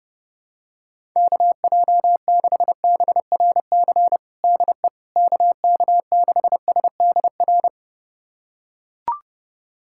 Morse code, heard as a keyed tone, about 32 wpm, 700 Hz. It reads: KJ6BRC DE KK6SDR
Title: USA Callsign DE Callsign - 6 Characters - 30wpm